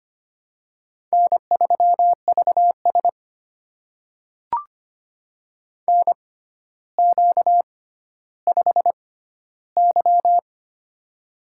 Morse code, heard as a keyed tone, about 25 wpm, spaced as for 13 wpm, 700 Hz. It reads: N3VS NQ5Y
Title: USA Call Signs - 4 Characters - 6x Character Spacing - 25wpm